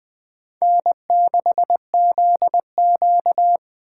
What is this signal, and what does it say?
Morse code, heard as a keyed tone, about 20 wpm, 700 Hz. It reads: N6ZQ
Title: Speed Racing - USA 1x2 Call Signs 17wpm